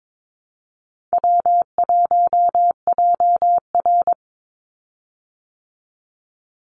Morse code, heard as a keyed tone, about 22 wpm, 700 Hz. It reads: W1JR
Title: Easy Call Signs Repeated 2x - 22wpm